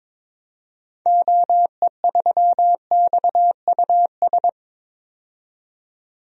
Morse code, heard as a keyed tone, about 22 wpm, 700 Hz. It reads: OE3XUS
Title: DXCC-100 Call Signs - 6 Characters - 22wpm